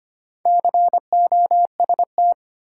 Morse code, heard as a keyed tone, about 25 wpm, 700 Hz. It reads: COST
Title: Sets of 3 Words - New Word Drill - 25wpm